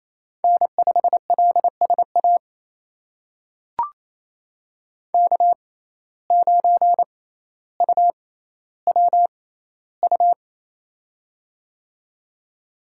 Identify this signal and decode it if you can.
Morse code, heard as a keyed tone, about 28 wpm, spaced as for 14 wpm, 700 Hz. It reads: N5LSA K9UWU
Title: USA Call Signs - 5 Characters - 6x Character Spacing - 28wpm